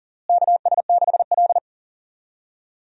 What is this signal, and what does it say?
Morse code, heard as a keyed tone, about 40 wpm, 700 Hz. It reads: KS6L
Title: USA Call Signs - 4 Characters - Repeated 2x - 40wpm